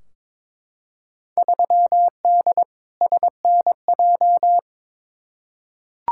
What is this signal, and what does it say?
Morse code, heard as a keyed tone, about 22 wpm, 700 Hz. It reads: 3D SNJ
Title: Field Day Exchanges Encore - 22wpm